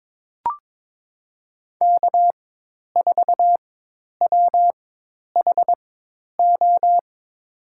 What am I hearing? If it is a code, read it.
Morse code, heard as a keyed tone, about 21 wpm, spaced as for 10 wpm, 700 Hz. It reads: K4WHO
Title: USA Call Signs - 4x Character Spacing - 22wpm